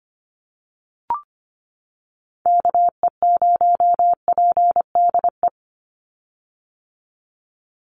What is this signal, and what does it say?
Morse code, heard as a keyed tone, about 25 wpm, 700 Hz. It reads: KE0PDE
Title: USA Call Signs 25wpm